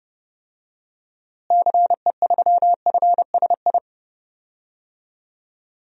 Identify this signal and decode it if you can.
Morse code, heard as a keyed tone, about 32 wpm, 700 Hz. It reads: CE3FSI
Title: DXCC-100 Call Signs - 6 Characters - Repeated 2x - 30wpm